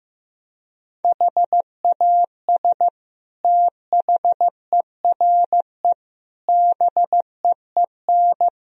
Morse code, heard as a keyed tone, about 15 wpm, 700 Hz. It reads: HAS THERE BEEN
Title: Sentences from Top 1000 Words - Review Part 4 - 15wpm